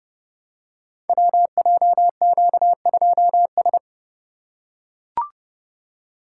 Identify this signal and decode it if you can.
Morse code, heard as a keyed tone, about 30 wpm, 700 Hz. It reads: WJQ2S
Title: Code Groups - 5 Letter:Numbers - 30wpm